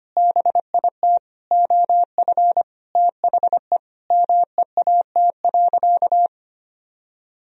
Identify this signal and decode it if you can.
Morse code, heard as a keyed tone, about 25 wpm, 700 Hz. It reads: BIT OF THE MEAT.
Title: Sentences from Top 1000 Words - Review Part 3 - 25wpm